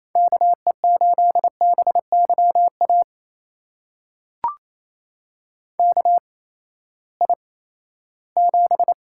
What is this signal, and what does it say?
Morse code, heard as a keyed tone, about 28 wpm, spaced as for 16 wpm, 700 Hz. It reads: KE8BYA KI7
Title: USA Call Signs - 6 Characters - 8x Character Spacing - 28wpm